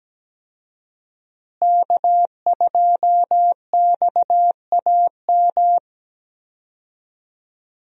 Morse code, heard as a keyed tone, about 17 wpm, 700 Hz. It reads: K2XAM
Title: USA Call Signs - 5 Characters - Repeated 3x - 17wpm